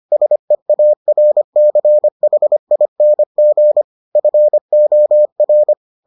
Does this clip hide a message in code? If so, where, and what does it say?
Yes: Morse code from 0.1 to 5.7 s, heard as a keyed tone, about 25 wpm, 600 Hz. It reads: SEARCHING FOR